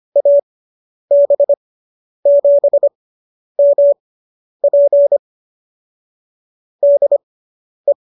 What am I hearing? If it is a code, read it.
Morse code, heard as a keyed tone, about 25 wpm, spaced as for 10 wpm, 590 Hz. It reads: AB7MP DE